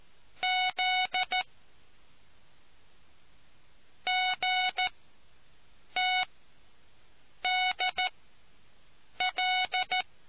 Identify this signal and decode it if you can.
Morse code, heard as a keyed tone, about 14 wpm, 740 Hz. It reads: Z G T D L